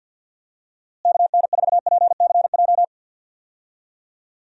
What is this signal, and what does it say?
Morse code, heard as a keyed tone, about 46 wpm, 700 Hz. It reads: KN4PCJ
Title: USA Call Signs - 6 Characters - Repeated 3x - 50wpm